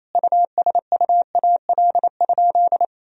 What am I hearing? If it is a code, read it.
Morse code, heard as a keyed tone, about 27 wpm, 700 Hz. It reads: USUAL?
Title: Sentences from Top 1000 Words - Review Part 2 - 28wpm